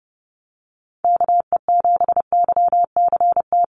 Morse code, heard as a keyed tone, about 28 wpm, 700 Hz. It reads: KE7YCT
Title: USA Call Signs Repeated 2x 30wpm